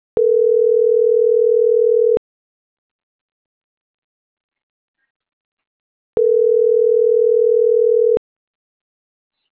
Ringback tone (North America), 440 + 480 Hz.